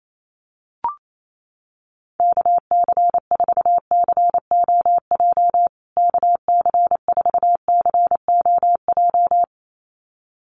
Morse code, heard as a keyed tone, about 28 wpm, 700 Hz. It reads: KC4COJ KC4COJ
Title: USA Call Signs Repeated 2x 28wpm